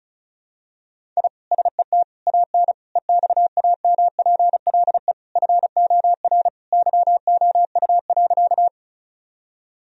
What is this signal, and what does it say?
Morse code, heard as a keyed tone, about 34 wpm, 700 Hz. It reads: I SET AN EXAMPLE FOR YOU.
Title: Sentences from Top 1000 Words - Review Part 1 - 35wpm